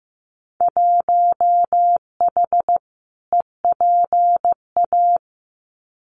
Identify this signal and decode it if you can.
Morse code, heard as a keyed tone, about 15 wpm, 700 Hz. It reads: 1H EPA
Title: Winter Field Day Exchanges 15wpm